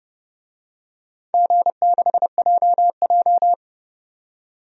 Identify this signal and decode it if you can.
Morse code, heard as a keyed tone, about 30 wpm, 700 Hz. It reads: G6JJ
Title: DXCC-100 Call Signs - 4 Characters - 30wpm